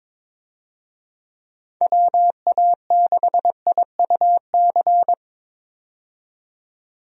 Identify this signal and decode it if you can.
Morse code, heard as a keyed tone, about 22 wpm, 700 Hz. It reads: WA6IUC
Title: USA Call Signs - 6 Characters - 22wpm